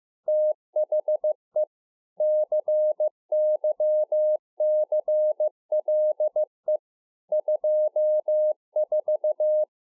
Morse code, heard as a keyed tone, about 15 wpm, 610 Hz. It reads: THE CYCLE 24
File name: 15WPM